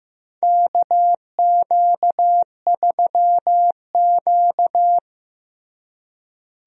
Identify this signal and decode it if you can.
Morse code, heard as a keyed tone, about 15 wpm, 700 Hz. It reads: KQ3Q
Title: USA Call Signs - 4 Characters - Repeated 3x - 15wpm